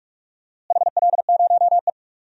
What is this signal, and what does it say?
Morse code, heard as a keyed tone, about 44 wpm, 700 Hz. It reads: HL0E
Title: DXCC-100 Call Signs - 4 Characters - Repeated 2x - 45wpm